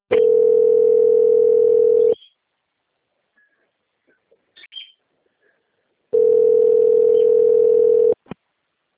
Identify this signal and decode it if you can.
Ringback tone (North America), 440 + 480 Hz.
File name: out- - - - - .9727